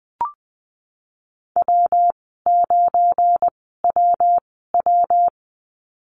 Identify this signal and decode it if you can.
Morse code, heard as a keyed tone, about 20 wpm, spaced as for 14 wpm, 700 Hz. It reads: W9WW
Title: 173 Lesson 28g - Number 9 - Call Signs 20wpm